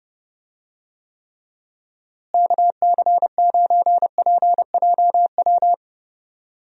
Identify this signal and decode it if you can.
Morse code, heard as a keyed tone, about 31 wpm, 700 Hz. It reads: KC9PJW
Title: USA Call Signs - 6 Characters - Repeated 3x - 30wpm